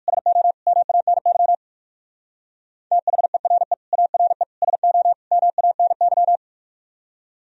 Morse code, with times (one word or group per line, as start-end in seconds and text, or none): 0.1-0.5 SO
0.7-1.5 MANY
2.9-3.7 THERE
3.9-4.4 ARE
4.6-5.1 SO
5.3-6.4 MANY